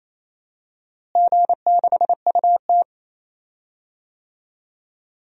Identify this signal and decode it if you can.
Morse code, heard as a keyed tone, about 28 wpm, 700 Hz. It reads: G6UT
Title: DXCC-100 Call Signs - 4 Characters - Repeated 3x - 28wpm